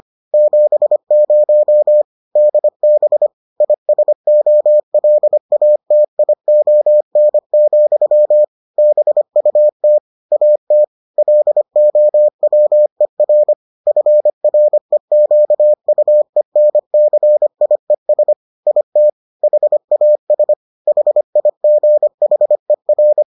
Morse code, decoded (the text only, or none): 70 DB ISOLATION, BUT AT LOWER FREQUENCIES IT HAS HIGHER